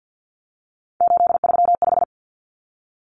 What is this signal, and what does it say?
Morse code, heard as a keyed tone, about 50 wpm, 700 Hz. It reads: 835
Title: Numbers - 3-Digit 50wpm